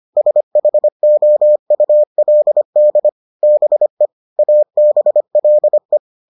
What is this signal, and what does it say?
Morse code, heard as a keyed tone, about 25 wpm, 600 Hz. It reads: SHOULD BE ABLE